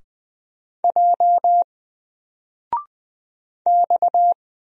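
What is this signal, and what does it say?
Morse code, heard as a keyed tone, about 20 wpm, 700 Hz. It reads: J X
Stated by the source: Single letter and number Morse drill at 20 wpm, from N4LCD.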